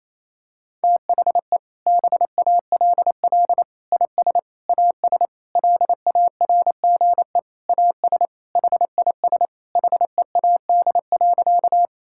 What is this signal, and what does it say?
Morse code, heard as a keyed tone, about 28 wpm, 700 Hz. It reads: THE BALL IS AS LARGE AS HIS HEAD.
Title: Sentences from Top 1000 Words - Review Part 6 - 28wpm